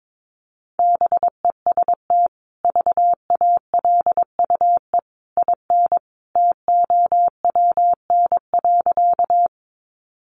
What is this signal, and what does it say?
Morse code, heard as a keyed tone, about 22 wpm, 700 Hz. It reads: BEST VALUE IN TOWN.
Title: Sentences from Top 600 Words 22wpm